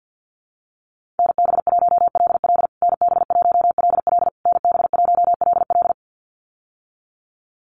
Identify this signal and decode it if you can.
Morse code, heard as a keyed tone, about 50 wpm, 700 Hz. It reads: NB1LL NB1LL NB1LL
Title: USA Call Signs Repeated 3x 50wpm